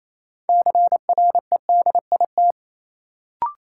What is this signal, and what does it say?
Morse code, heard as a keyed tone, about 28 wpm, 700 Hz. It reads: CREDIT
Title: Sentences from Top 1000 Words - New Word Drill - 28wpm